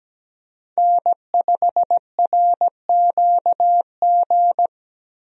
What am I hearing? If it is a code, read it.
Morse code, heard as a keyed tone, about 17 wpm, 700 Hz. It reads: N5RQG
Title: USA Call Signs - 5 Characters - Repeated 3x - 17wpm